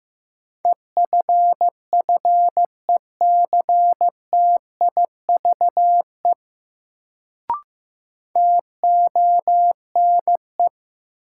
Morse code, heard as a keyed tone, about 15 wpm, 700 Hz. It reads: EFFECTIVE TONE
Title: Sentences from Top 1600 Words - New Word Drill Alt - 15wpm